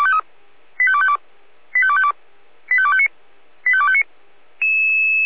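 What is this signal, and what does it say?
Five-tone selective call (ZVEI), heard as quick runs of five tones, tones of about 70 ms. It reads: 85252 85252 85259 85259
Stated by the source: ZVEI five-tone selective call 85250 received by SDR at feuerwehr-orlamuende.org, fire brigade.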